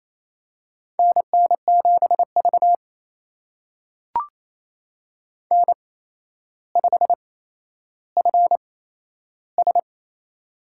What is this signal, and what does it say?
Morse code, heard as a keyed tone, about 27 wpm, spaced as for 12 wpm, 700 Hz. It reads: NN7V N5FS
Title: USA Call Signs - 4 Characters - 8x Character Spacing - 28wpm